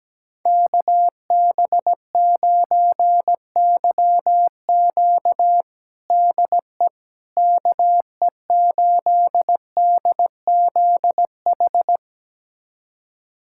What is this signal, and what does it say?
Morse code, heard as a keyed tone, about 17 wpm, 700 Hz. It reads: KB9YQ DE KE8DZH
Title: USA Callsign DE Callsign 17wpm